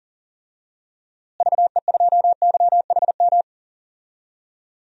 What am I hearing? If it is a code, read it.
Morse code, heard as a keyed tone, about 40 wpm, 700 Hz. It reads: VE2YHM
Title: DXCC-100 Call Signs - 6 Characters - Repeated 2x - 40wpm